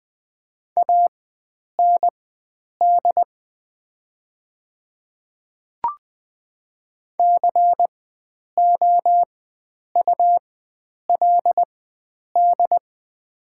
Morse code, heard as a keyed tone, about 20 wpm, spaced as for 9 wpm, 700 Hz. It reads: AND COULD
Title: 4x Character Spacing - Top 100 Words - 20wpm